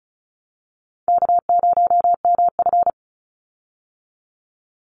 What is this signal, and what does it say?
Morse code, heard as a keyed tone, about 34 wpm, 700 Hz. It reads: K0MF
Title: USA Call Signs 35wpm